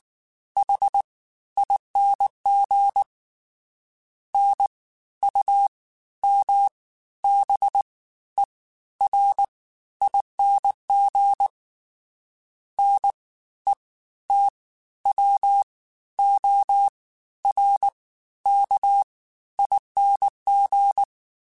Morse code, heard as a keyed tone, about 19 wpm, spaced as for 12 wpm, 800 Hz. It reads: HING NUMBERING NETWORKING